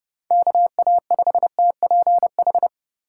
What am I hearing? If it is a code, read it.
Morse code, heard as a keyed tone, about 32 wpm, 700 Hz. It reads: KA5TPH
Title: USA Call Signs - 6 Characters - 3x Character Spacing - 30wpm